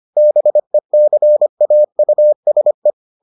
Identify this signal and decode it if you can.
Morse code, heard as a keyed tone, about 25 wpm, 600 Hz. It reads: BECAUSE